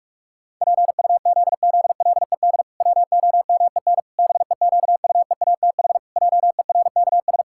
Morse code, heard as a keyed tone, about 46 wpm, 700 Hz. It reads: PUZZLED WOMEN BEQUEATH JERKS
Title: Pangram Challenge v2 - 45wpm